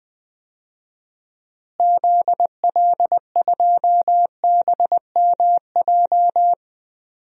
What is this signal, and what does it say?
Morse code, heard as a keyed tone, about 20 wpm, 700 Hz. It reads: ZL2BMJ
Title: DXCC-100 Call Signs - 6 Characters - Repeated 3x - 20wpm